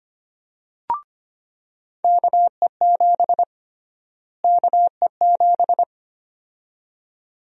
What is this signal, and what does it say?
Morse code, heard as a keyed tone, about 25 wpm, 700 Hz. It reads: KE7 KE7
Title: USA Call Sign Prefixes - Repeated 2x - 25wpm